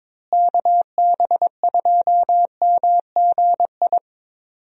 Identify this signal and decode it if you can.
Morse code, heard as a keyed tone, about 22 wpm, 700 Hz. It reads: KB2MGI
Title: USA Call Signs - 6 Characters - Repeated 2x - 22wpm